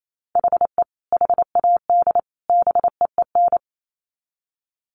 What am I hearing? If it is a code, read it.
Morse code, heard as a keyed tone, about 28 wpm, 700 Hz. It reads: HE HAD BEEN
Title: Sets of 3 Words 28wpm